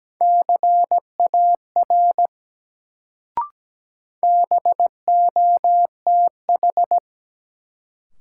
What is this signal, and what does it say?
Morse code, heard as a keyed tone, about 17 wpm, 700 Hz. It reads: CAR BOTH
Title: Sentences from Top 300 Words - New Word Drill - 17wpm